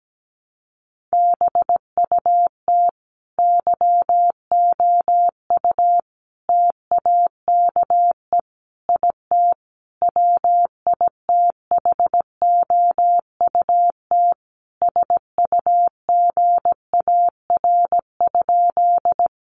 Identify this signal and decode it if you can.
Morse code, heard as a keyed tone, about 17 wpm, 700 Hz. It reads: BUT YOU TAKE IT WITHOUT SUGAR?
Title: Sentences from Top 1200 Words 17wpm